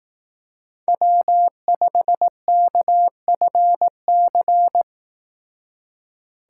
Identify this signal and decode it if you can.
Morse code, heard as a keyed tone, about 18 wpm, 700 Hz. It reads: W5KFC
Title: Speed Racing - USA 1x3 Call Signs 15wpm